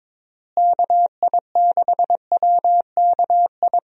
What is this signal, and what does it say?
Morse code, heard as a keyed tone, about 22 wpm, 700 Hz. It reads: KI6WKI
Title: USA Call Signs - 6 Characters - Repeated 3x - 22wpm